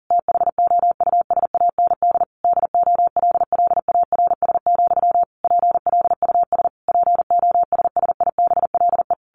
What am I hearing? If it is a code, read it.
Morse code, heard as a keyed tone, about 40 wpm, 700 Hz. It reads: THOUSAND DOLLARS, PLUS POSSIBLE
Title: A Sound Of Thunder 40wpm